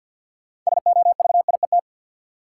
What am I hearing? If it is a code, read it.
Morse code, heard as a keyed tone, about 52 wpm, 700 Hz. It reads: SOVIET
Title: Sets of 5 Words Encore - New Word Drill - 50wpm